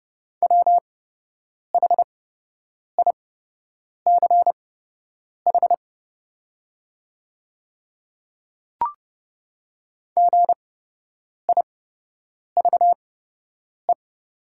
Morse code, heard as a keyed tone, about 30 wpm, spaced as for 8 wpm, 700 Hz. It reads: WHICH GIVE
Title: 8x Character Spacing - Top 300 Words - 30wpm